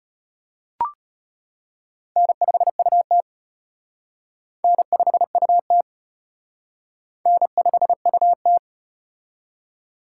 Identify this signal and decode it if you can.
Morse code, heard as a keyed tone, about 33 wpm, 700 Hz. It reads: N5UT N5UT N5UT